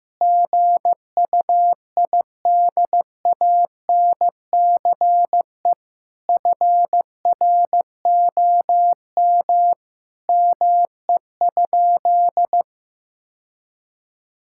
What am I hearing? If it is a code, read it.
Morse code, heard as a keyed tone, about 15 wpm, 700 Hz. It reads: GUIDANCE FROM ME?
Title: Sentences from Top 1600 Words 15wpm